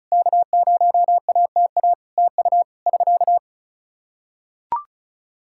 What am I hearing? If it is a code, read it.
Morse code, heard as a keyed tone, about 36 wpm, 700 Hz. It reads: K0ATA TU <SK>
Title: Practice QSO - Exchange 4 - Ending - 35wpm